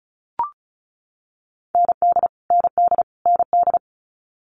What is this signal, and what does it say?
Morse code, heard as a keyed tone, about 36 wpm, 700 Hz. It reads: ND ND ND
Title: Multi-letter Phonemes Repeated 3x 35wpm